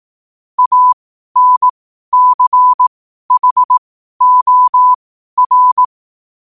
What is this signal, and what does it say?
Morse code, heard as a keyed tone, about 18 wpm, spaced as for 12 wpm, 1000 Hz. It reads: ANCHOR
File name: qotdfiles / 12wpm